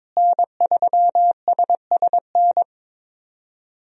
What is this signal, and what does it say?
Morse code, heard as a keyed tone, about 22 wpm, 700 Hz. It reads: N3SSN
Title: USA Call Signs - 5 Characters - Repeated 3x - 22wpm